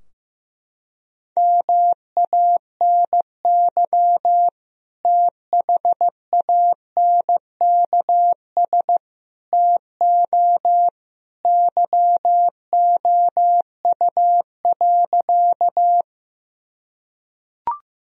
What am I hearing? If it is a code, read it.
Morse code, heard as a keyed tone, about 15 wpm, 700 Hz. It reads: MANY THANKS TO YOU.
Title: Sentences from Top 1000 Words - Review Part 3 - 15wpm